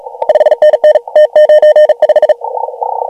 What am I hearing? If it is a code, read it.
Morse code, heard as a keyed tone, about 38 wpm, 610 Hz. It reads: 5NN T95
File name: AUDIO2019 / HG1G CW